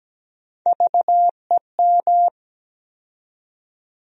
Morse code, heard as a keyed tone, about 17 wpm, 700 Hz. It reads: VEM
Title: USA Call Sign Postfixes - 3 Letters 17wpm